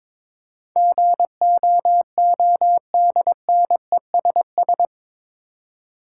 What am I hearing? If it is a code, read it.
Morse code, heard as a keyed tone, about 22 wpm, 700 Hz. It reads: GOODNESS
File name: practice / 22wpm / Eight Letter Words - 22wpm